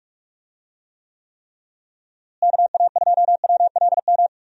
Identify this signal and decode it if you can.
Morse code, heard as a keyed tone, about 44 wpm, 700 Hz. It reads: KA2WLM
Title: USA Call Signs - 6 Characters - Repeated 3x - 45wpm